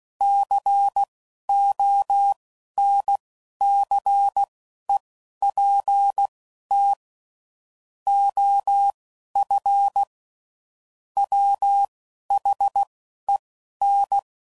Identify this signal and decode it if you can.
Morse code, heard as a keyed tone, about 16 wpm, spaced as for 11 wpm, 800 Hz. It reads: CONCEPT OF WHEN
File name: CW Code Groups @10WPM/Technical Subjects